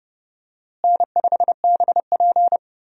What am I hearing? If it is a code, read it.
Morse code, heard as a keyed tone, about 30 wpm, 700 Hz. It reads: N5BP
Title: USA Call Signs - 4 Characters - Repeated 3x - 30wpm